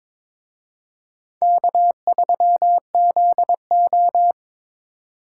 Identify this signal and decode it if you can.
Morse code, heard as a keyed tone, about 26 wpm, 700 Hz. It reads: K3ZO
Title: Speed Racing - USA 1x2 Call Signs 22wpm